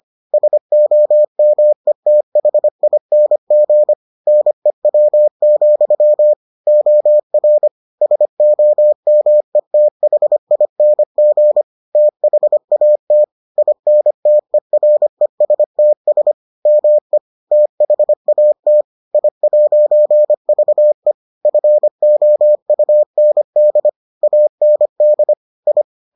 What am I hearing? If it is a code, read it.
Morse code, heard as a keyed tone, about 25 wpm, 600 Hz. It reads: SOMETHING NEW, OR SOMETHING THAT INTERESTS ME THAT I'VE FOUND AND I